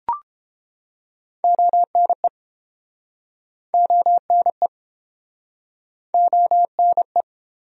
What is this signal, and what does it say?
Morse code, heard as a keyed tone, about 28 wpm, 700 Hz. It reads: ONE ONE ONE